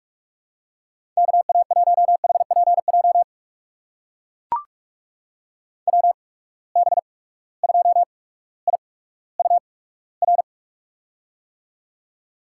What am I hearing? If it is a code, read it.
Morse code, heard as a keyed tone, about 44 wpm, spaced as for 19 wpm, 700 Hz. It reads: KA1HPJ WB2IUR
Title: USA Call Signs - 6 Characters - 8x Character Spacing - 45wpm